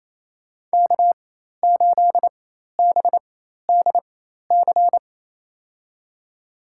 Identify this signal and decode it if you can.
Morse code, heard as a keyed tone, about 28 wpm, spaced as for 13 wpm, 700 Hz. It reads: K8BDC
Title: USA Call Signs - 5 Characters - 4x Character Spacing - 28wpm